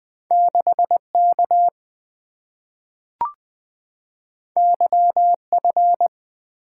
Morse code, heard as a keyed tone, about 20 wpm, 700 Hz. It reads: 6K YF
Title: Two Letter-Number Combinations - 20wpm